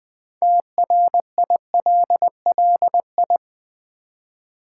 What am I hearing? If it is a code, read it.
Morse code, heard as a keyed tone, about 20 wpm, 700 Hz. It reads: TRILLI
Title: Build A Word - 2 Letters At A Time - 8-Letter Words - 20wpm